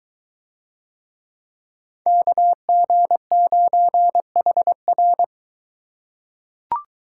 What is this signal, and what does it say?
Morse code, heard as a keyed tone, about 22 wpm, 700 Hz. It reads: KG9HR